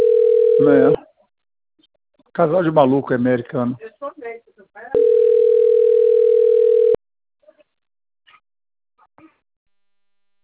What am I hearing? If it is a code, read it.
Ringback tone (North America), 440 + 480 Hz, with speech.
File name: rg-1600-unknown- - - .55760